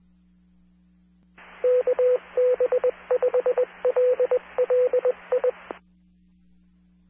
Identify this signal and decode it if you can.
Morse code, heard as a keyed tone, about 20 wpm, 500 Hz, with radio noise. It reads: KB5LLI